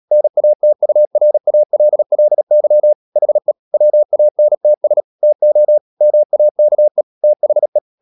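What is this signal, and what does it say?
Morse code, heard as a keyed tone, about 37 wpm, 600 Hz. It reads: NATURALLY HE WANTS TO MAKE THE
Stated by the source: Morse practice at 36 wpm, text of Ken Follett's novel Never, from CW med Gnister.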